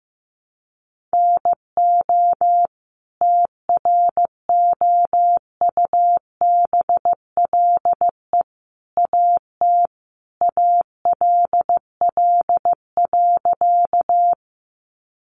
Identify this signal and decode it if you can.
Morse code, heard as a keyed tone, about 15 wpm, 700 Hz. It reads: NO TROUBLE AT ALL.
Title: Sentences from Top 1200 Words 15wpm